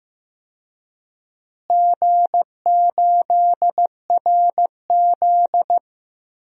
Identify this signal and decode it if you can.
Morse code, heard as a keyed tone, about 15 wpm, 700 Hz. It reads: G8RZ
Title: DXCC-100 Call Signs - 4 Characters - Repeated 3x - 15wpm